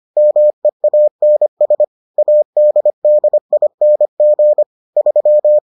Morse code, heard as a keyed tone, about 25 wpm, 600 Hz. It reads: MEANS ADDING 3